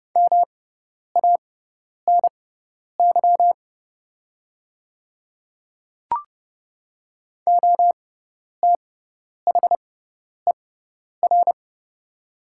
Morse code, heard as a keyed tone, about 30 wpm, spaced as for 10 wpm, 700 Hz. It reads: MANY OTHER
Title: 6x Character Spacing - Top 100 Words - 30wpm